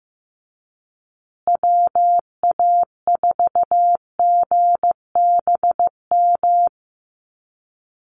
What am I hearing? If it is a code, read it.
Morse code, heard as a keyed tone, about 15 wpm, 700 Hz. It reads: WA4GBM
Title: USA Call Signs Repeated 3x 15wpm